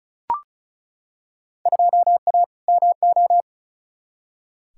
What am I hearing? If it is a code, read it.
Morse code, heard as a keyed tone, about 36 wpm, 700 Hz. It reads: 2A MO
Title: Field Day Exchanges Encore - 35wpm